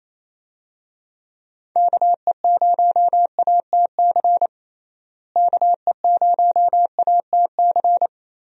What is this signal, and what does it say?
Morse code, heard as a keyed tone, about 28 wpm, 700 Hz. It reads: KE0ATC KE0ATC
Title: USA Call Signs - 6 Characters - Repeated 3x - 28wpm